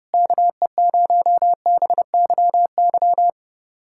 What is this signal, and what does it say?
Morse code, heard as a keyed tone, about 30 wpm, 700 Hz. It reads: KE0BYY
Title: USA Callsign DE Callsign - 6 Characters - 30wpm